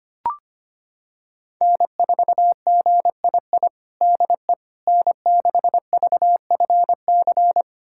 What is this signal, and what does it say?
Morse code, heard as a keyed tone, about 25 wpm, 700 Hz. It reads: N4GII DE N6VFC
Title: USA Callsign DE Callsign - 5 Characters - 25wpm